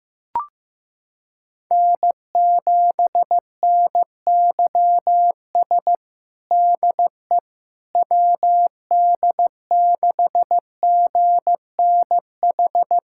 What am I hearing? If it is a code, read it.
Morse code, heard as a keyed tone, about 15 wpm, 700 Hz. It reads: N7NYS DE WD6GNH
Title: USA Callsign DE Callsign 15wpm